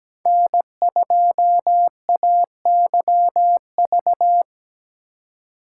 Morse code, heard as a keyed tone, about 17 wpm, 700 Hz. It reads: N2AYV
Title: USA Call Signs - 5 Characters - 3x Character Spacing - 17wpm